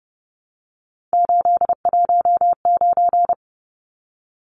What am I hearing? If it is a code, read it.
Morse code, heard as a keyed tone, about 30 wpm, 700 Hz. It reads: 819
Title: Numbers - 3-Digit 30wpm